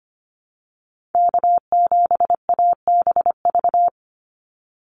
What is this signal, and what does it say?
Morse code, heard as a keyed tone, about 25 wpm, 700 Hz. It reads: K7ABV
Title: Call Signs Repeated 3x - 25wpm